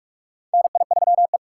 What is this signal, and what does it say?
Morse code, heard as a keyed tone, about 46 wpm, 700 Hz. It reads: NI3E
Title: USA Call Signs - 4 Characters - 2x Character Spacing - 45wpm